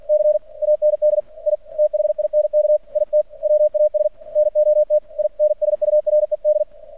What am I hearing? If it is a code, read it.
Morse code, heard as a keyed tone, about 48 wpm, 600 Hz. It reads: C AND I THINK IT WAS NOT INSURED
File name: archived / CRSnet